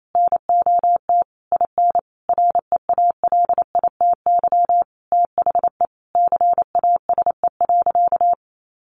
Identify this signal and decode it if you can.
Morse code, heard as a keyed tone, about 28 wpm, 700 Hz. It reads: NOT IN REALITY THE CASE.
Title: Sentences from Top 1200 Words 28wpm